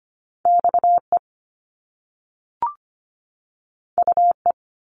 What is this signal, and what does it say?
Morse code, heard as a keyed tone, about 25 wpm, 700 Hz. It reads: XE UE
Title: Top Two Letter Combinations in Words 25wpm